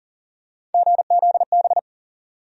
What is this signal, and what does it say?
Morse code, heard as a keyed tone, about 40 wpm, 700 Hz. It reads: GZB
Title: USA Call Sign Postfixes - 3 Letters 40wpm